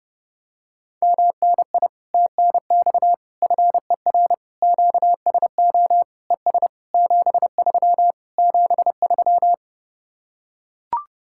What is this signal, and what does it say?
Morse code, heard as a keyed tone, about 29 wpm, 700 Hz. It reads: MNI TNX FER QSO ES 73 73